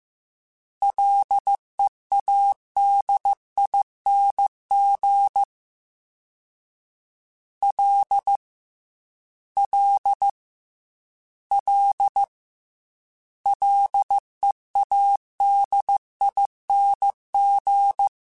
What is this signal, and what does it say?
Morse code, heard as a keyed tone, about 15 wpm, 800 Hz. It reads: LEADING L L L LEADING
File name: F and L / F and L mp3 / F and L Prefixes 3x2 Twice @13